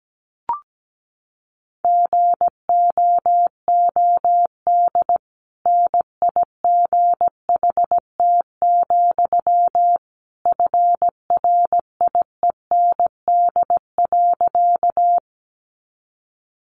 Morse code, heard as a keyed tone, about 17 wpm, 700 Hz. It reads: GOOD NIGHT, FRIEND.
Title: Sentences from Top 300 Words 17wpm